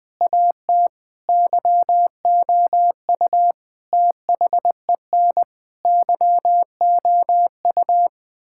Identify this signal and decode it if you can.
Morse code, heard as a keyed tone, about 20 wpm, 700 Hz. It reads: AT YOU THEN YOU
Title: Sentences with 8-Word Phrases - 20wpm